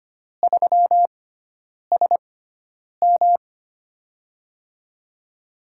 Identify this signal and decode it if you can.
Morse code, heard as a keyed tone, about 25 wpm, spaced as for 9 wpm, 700 Hz. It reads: 3SM
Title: USA Call Signs - 4 Characters - 6x Character Spacing - 25wpm